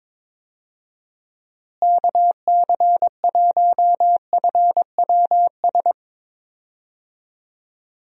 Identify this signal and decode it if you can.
Morse code, heard as a keyed tone, about 22 wpm, 700 Hz. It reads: KC1FWS
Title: USA Call Signs - 6 Characters - Repeated 2x - 22wpm